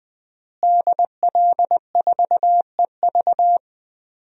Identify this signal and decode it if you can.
Morse code, heard as a keyed tone, about 21 wpm, 700 Hz. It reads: DL4EV
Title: DXCC-100 Call Signs - 5 Characters - Repeated 3x - 20wpm